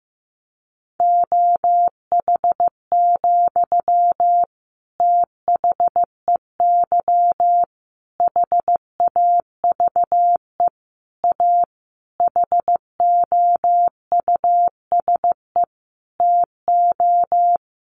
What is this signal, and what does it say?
Morse code, heard as a keyed tone, about 15 wpm, 700 Hz. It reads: OH, THEY HAVE A HOUSE TO
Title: Sentences from Top 800 Words 15wpm